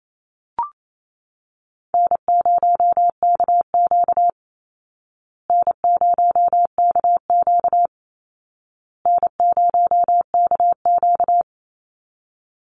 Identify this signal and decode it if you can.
Morse code, heard as a keyed tone, about 28 wpm, 700 Hz. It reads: N0KQ N0KQ N0KQ
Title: Easy Call Signs Repeated 3x - 28wpm